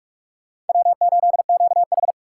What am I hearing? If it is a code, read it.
Morse code, heard as a keyed tone, about 46 wpm, 700 Hz. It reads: W8QH
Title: USA Call Signs - 4 Characters - 2x Character Spacing - 45wpm